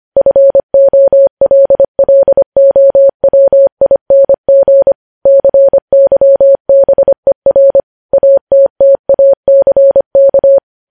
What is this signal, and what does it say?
Morse code, heard as a keyed tone, about 25 wpm, 550 Hz. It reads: FOLLOWING CYBER ATTACK
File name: podcast / News headlines-1x-25x25